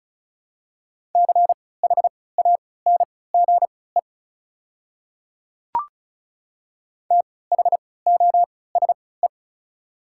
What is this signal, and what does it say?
Morse code, heard as a keyed tone, about 36 wpm, spaced as for 20 wpm, 690 Hz. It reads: CHANGE THOSE
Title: 3x Character Spacing - Top 300 Words - 35wpm